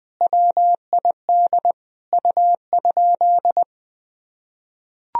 Morse code, heard as a keyed tone, about 21 wpm, 700 Hz. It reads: WID U?
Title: Sentences from Top 500 Words and Abbreviations 20wpm